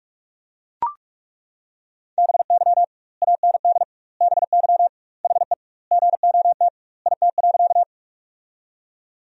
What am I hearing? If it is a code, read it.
Morse code, heard as a keyed tone, about 42 wpm, 700 Hz. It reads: BY AND BY HE GOT IT.
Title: 2x Word-Spacing - Sentences from Top 300 Words - 45wpm